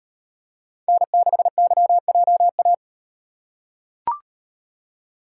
Morse code, heard as a keyed tone, about 37 wpm, 700 Hz. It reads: N6YJA